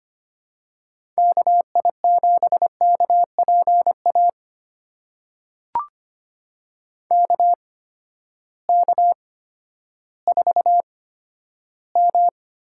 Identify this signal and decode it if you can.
Morse code, heard as a keyed tone, about 25 wpm, spaced as for 12 wpm, 700 Hz. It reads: KI7KPA KK4M
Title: USA Call Signs - 6 Characters - 8x Character Spacing - 25wpm